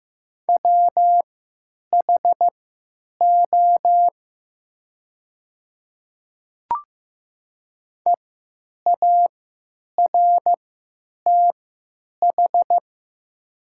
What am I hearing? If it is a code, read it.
Morse code, heard as a keyed tone, about 15 wpm, spaced as for 9 wpm, 700 Hz. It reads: WHO EARTH
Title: 3x Character Spacing - Top 200 Words - 15wpm